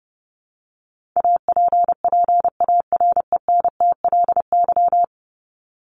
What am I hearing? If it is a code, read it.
Morse code, heard as a keyed tone, about 30 wpm, 700 Hz. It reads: APPARENTLY
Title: LY Suffix - Top 100 Words with Base Words 30wpm